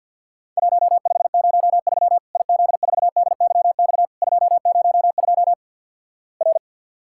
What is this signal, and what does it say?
Morse code, heard as a keyed tone, about 50 wpm, 700 Hz. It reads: 1503 IZ4DYX 202 R
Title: Worked All Europe DX Contest - QTC Exchange - 50wpm